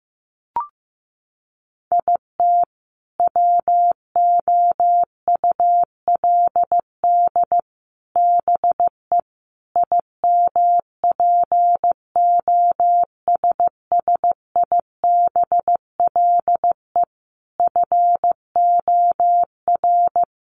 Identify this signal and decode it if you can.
Morse code, heard as a keyed tone, about 15 wpm, 700 Hz. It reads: IT WOULD BE IMPOSSIBLE FOR
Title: Sentences from Top 1100 Words 15wpm